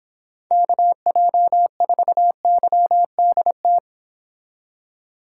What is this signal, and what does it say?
Morse code, heard as a keyed tone, about 26 wpm, 700 Hz. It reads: KJ4YDT